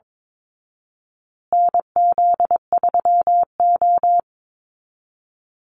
Morse code, heard as a keyed tone, about 22 wpm, 700 Hz. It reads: NZ3O
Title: Call Signs Repeated 3x - 22wpm